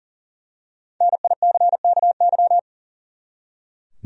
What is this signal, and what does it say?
Morse code, heard as a keyed tone, about 40 wpm, 700 Hz. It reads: NICKY
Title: USA Common Names - Max 5 Letters - 40wpm